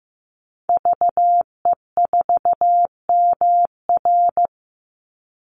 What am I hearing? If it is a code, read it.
Morse code, heard as a keyed tone, about 15 wpm, 700 Hz. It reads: VE4MR
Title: Call Signs Repeated 3x - 15wpm